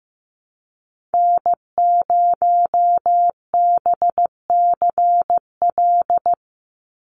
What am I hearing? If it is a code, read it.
Morse code, heard as a keyed tone, about 15 wpm, 700 Hz. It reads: N0BCL
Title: USA Call Signs 15wpm